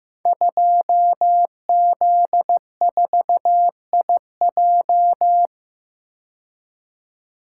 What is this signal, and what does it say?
Morse code, heard as a keyed tone, about 15 wpm, 700 Hz. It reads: 2Z4IJ
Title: Code Groups - 5 Letter:Numbers - 15wpm